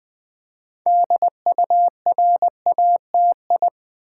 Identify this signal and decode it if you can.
Morse code, heard as a keyed tone, about 20 wpm, 700 Hz. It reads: DURATI
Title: Build A Word - 2 Letters At A Time - 8-Letter Words - 20wpm